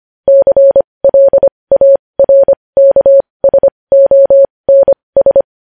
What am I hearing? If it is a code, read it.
Morse code, heard as a keyed tone, about 25 wpm, spaced as for 20 wpm, 550 Hz. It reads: CLARKSONS
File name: podcast / News headlines-1x-25x20